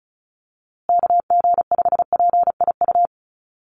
Morse code, heard as a keyed tone, about 34 wpm, 700 Hz. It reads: KG5PIU
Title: USA Call Signs Repeated 3x 35wpm